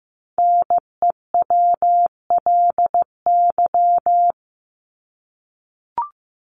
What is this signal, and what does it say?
Morse code, heard as a keyed tone, about 15 wpm, 700 Hz. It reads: NEWLY
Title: LY Suffix - Top 100 Words with Base Words 15wpm